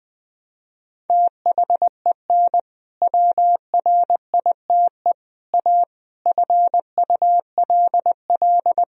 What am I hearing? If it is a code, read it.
Morse code, heard as a keyed tone, about 20 wpm, 700 Hz. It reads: THEN WRITE A FULL
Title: Sentences from Top 1000 Words - Review Part 3 - 20wpm